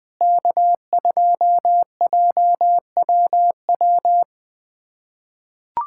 Morse code, heard as a keyed tone, about 20 wpm, 700 Hz. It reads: K2JWW
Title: SOTA QSO Practice 20wpm